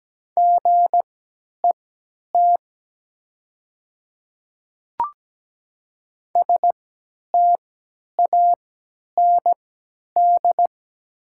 Morse code, heard as a keyed tone, about 17 wpm, spaced as for 10 wpm, 700 Hz. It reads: GET STAND
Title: 3x Character Spacing - Top 300 Words - 17wpm